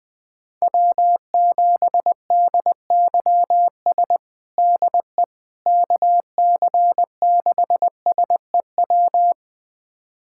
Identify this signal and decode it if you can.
Morse code, heard as a keyed tone, about 20 wpm, 700 Hz. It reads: W7DYS DE KC6SEW
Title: USA Callsign DE Callsign 20wpm